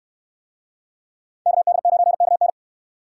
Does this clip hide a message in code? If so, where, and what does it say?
Yes: Morse code from 1.5 to 2.5 s, heard as a keyed tone, about 66 wpm, 700 Hz. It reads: KD9KN